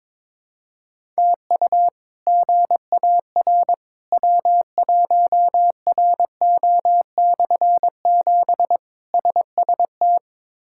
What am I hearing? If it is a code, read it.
Morse code, heard as a keyed tone, about 22 wpm, 700 Hz. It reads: TU GAR W1RO/7 SST
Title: K1USN SST QSO Practice - 22wpm